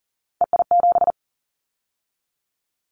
Morse code, heard as a keyed tone, about 40 wpm, 700 Hz. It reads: EI7
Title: Callsign Prefixes for Top 50 DXCC - 40wpm